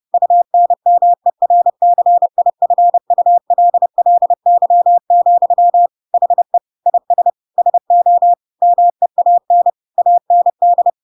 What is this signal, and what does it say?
Morse code, heard as a keyed tone, about 30 wpm, 700 Hz. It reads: UNMERCIFULLY, HE IS SO MEAN AND
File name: Pinocchio / Pinocchio ch3 30 30 ik1hge